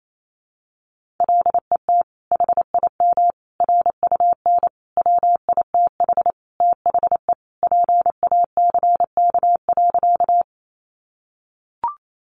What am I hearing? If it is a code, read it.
Morse code, heard as a keyed tone, about 28 wpm, 700 Hz. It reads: LET HIM RUN WITH THE PACK.
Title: Sentences from Top 1300 Words 28wpm